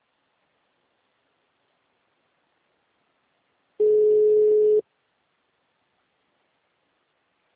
Ringback tone (Europe, 425 Hz).